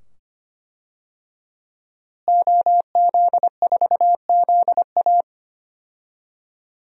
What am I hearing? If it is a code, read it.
Morse code, heard as a keyed tone, about 25 wpm, 700 Hz. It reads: OZ4ZA
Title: DXCC-100 Call Signs - 5 Characters - Repeated 2x - 25wpm